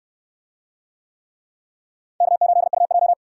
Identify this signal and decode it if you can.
Morse code, heard as a keyed tone, about 68 wpm, 700 Hz. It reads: K8UJ